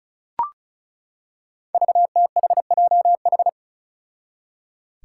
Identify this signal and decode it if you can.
Morse code, heard as a keyed tone, about 36 wpm, 700 Hz. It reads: VTHJH
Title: Five Challenging Letter-Number Combinations 35wpm